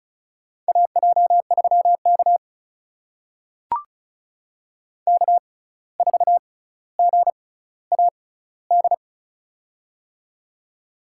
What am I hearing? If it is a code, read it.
Morse code, heard as a keyed tone, about 34 wpm, spaced as for 17 wpm, 700 Hz. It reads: AJ3K K4GAD
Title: USA Call Signs - 6x Character Spacing - 35wpm